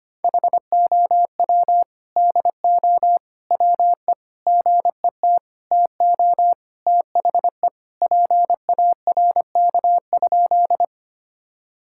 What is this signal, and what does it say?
Morse code, heard as a keyed tone, about 25 wpm, 700 Hz. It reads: HOW DO WE GET TO THE PARK?
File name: practice / 25wpm / Sentences with 4-Word Phrases - 25wpm